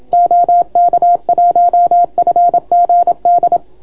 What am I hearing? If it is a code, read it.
Morse code, heard as a keyed tone, about 27 wpm, 690 Hz. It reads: OK1FGD